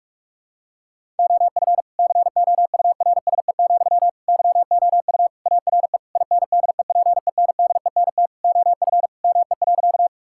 Morse code, heard as a keyed tone, about 46 wpm, 700 Hz. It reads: OF COURSE, YOU ARE INDEPENDENT OF ME.
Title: Sentences from Top 1000 Words - Review Part 6 - 45wpm